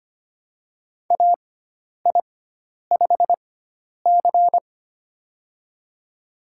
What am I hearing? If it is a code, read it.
Morse code, heard as a keyed tone, about 25 wpm, spaced as for 10 wpm, 700 Hz. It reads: AI5C
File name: Lesson 16g - Letter c - Call Signs 25wpm